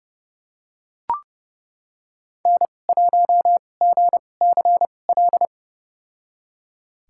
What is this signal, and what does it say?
Morse code, heard as a keyed tone, about 30 wpm, spaced as for 22 wpm, 700 Hz. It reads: N1GCL
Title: USA Call Signs - 5 Characters - 2x Character Spacing - 30wpm